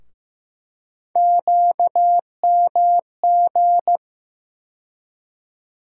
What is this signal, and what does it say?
Morse code, heard as a keyed tone, about 15 wpm, 700 Hz. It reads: QMG
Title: USA Call Sign Postfixes - 3 Letters 15wpm